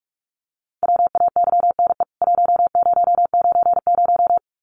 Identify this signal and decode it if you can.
Morse code, heard as a keyed tone, about 44 wpm, 700 Hz. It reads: WAYNE 1090
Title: CWT Exchange 45wpm